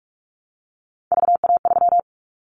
Morse code, heard as a keyed tone, about 44 wpm, 700 Hz. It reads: VA3
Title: Callsign Prefixes for Top 100 DXCC - 45wpm